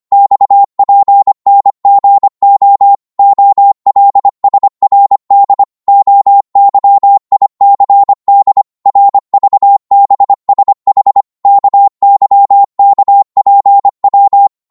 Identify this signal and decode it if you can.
Morse code, heard as a keyed tone, about 25 wpm, 800 Hz. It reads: XPNGO OLSRD OYICD RVBSH KYKPW
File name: Morse / ltr-25wpm